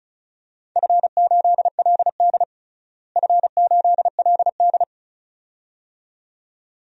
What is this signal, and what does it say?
Morse code, heard as a keyed tone, about 36 wpm, 690 Hz. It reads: F8LD F8LD
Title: DXCC-100 Call Signs - 4 Characters - Repeated 2x - 35wpm